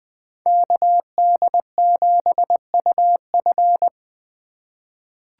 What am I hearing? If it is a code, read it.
Morse code, heard as a keyed tone, about 20 wpm, 700 Hz. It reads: KD7UF
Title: USA Call Signs - 5 Characters - 4x Character Spacing - 20wpm